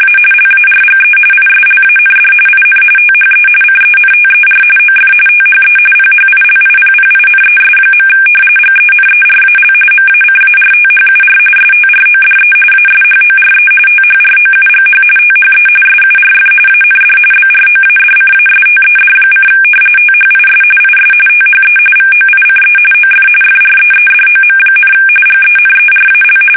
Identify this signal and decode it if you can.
RTTY, about 45.45 baud. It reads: A LARGE ISLAND IN THE CENTER OF THE PHILIPPINES. GENERAL MACARTHUR STRODE ASHORE THROUGH THE SURF AND BROADCASTED: THIS IS THE VOICE OF FREEDOM, GENERAL
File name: fsk850 / battleshipnj